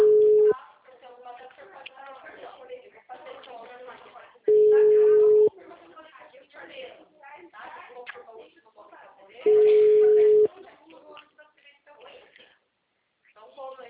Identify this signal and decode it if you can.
Ringback tone (Europe, 425 Hz), with speech.